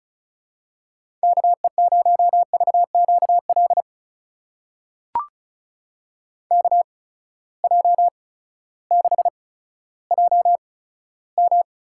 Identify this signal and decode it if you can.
Morse code, heard as a keyed tone, about 36 wpm, spaced as for 16 wpm, 700 Hz. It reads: KE0VQL KJ6JM
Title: USA Call Signs - 6 Characters - 8x Character Spacing - 35wpm